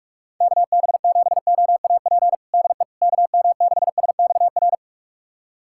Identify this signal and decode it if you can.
Morse code, heard as a keyed tone, about 46 wpm, 700 Hz. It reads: KB7OAP DE KM6SXR
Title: USA Callsign DE Callsign - 6 Characters - 45wpm